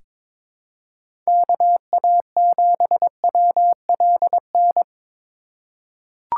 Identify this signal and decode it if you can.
Morse code, heard as a keyed tone, about 22 wpm, 700 Hz. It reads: KA7WLN
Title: USA Call Signs - 3x Character Spacing - 22wpm